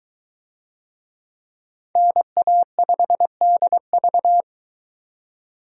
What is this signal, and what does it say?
Morse code, heard as a keyed tone, about 21 wpm, 700 Hz. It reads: NA5DV